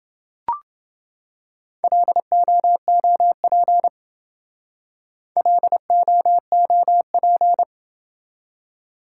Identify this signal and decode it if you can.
Morse code, heard as a keyed tone, about 28 wpm, 700 Hz. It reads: LOOP LOOP